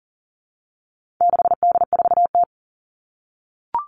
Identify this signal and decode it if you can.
Morse code, heard as a keyed tone, about 40 wpm, 700 Hz. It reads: 6D4T
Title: All Cut Numbers - 4-Digit 40wpm